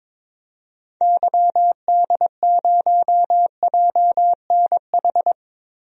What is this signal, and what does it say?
Morse code, heard as a keyed tone, about 22 wpm, 700 Hz. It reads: YD0JNH
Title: DXCC-100 Call Signs - 6 Characters - 22wpm